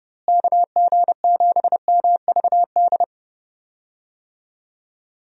Morse code, heard as a keyed tone, about 32 wpm, 700 Hz. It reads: KG7MVD
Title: USA Call Signs - 6 Characters - Repeated 2x - 30wpm